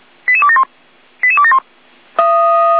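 Five-tone selective call (ZVEI), heard as quick runs of five tones, tones of about 70 ms. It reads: 88271 88271
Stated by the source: Five-tone ZVEI selective call received on an SDR, 455.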